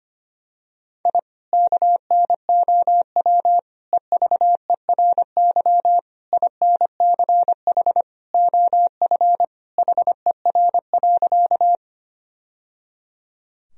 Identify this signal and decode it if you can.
Morse code, heard as a keyed tone, about 25 wpm, 700 Hz. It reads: I KNOW EVERY INCH OF HER.
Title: Sentences from Top 1000 Words - Review Part 3 - 25wpm